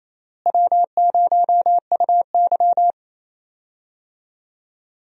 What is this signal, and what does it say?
Morse code, heard as a keyed tone, about 28 wpm, 700 Hz. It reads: W0UY
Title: USA Call Signs - 4 Characters - Repeated 3x - 28wpm